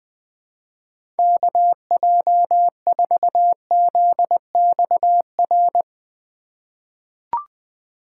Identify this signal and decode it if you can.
Morse code, heard as a keyed tone, about 20 wpm, 700 Hz. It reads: KJ4ZXR
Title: USA Call Signs - 6 Characters - 4x Character Spacing - 20wpm